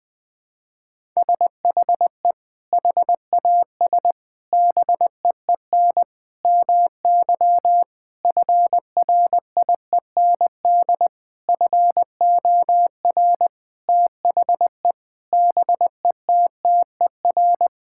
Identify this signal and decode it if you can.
Morse code, heard as a keyed tone, about 21 wpm, 700 Hz. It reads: SHE HAS BEEN MY FRIEND FOR THE BETTER
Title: Sentences with 6-Word Phrases - 20wpm